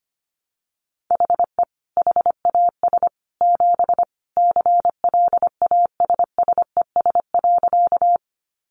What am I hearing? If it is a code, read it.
Morse code, heard as a keyed tone, about 25 wpm, 700 Hz. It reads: HE HAS 7 CLASSES.
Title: Additional Easy Phrases 25wpm